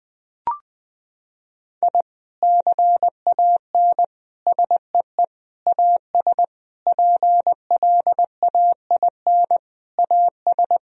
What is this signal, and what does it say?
Morse code, heard as a keyed tone, about 21 wpm, 700 Hz. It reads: I CAN SEE AS PLAIN AS
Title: Sentences from Top 1000 Words - Review Part 2 - 20wpm